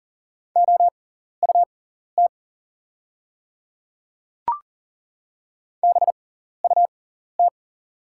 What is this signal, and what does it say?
Morse code, heard as a keyed tone, about 38 wpm, spaced as for 14 wpm, 700 Hz. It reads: OUT BUT
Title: 6x Character Spacing - Top 100 Words - 40wpm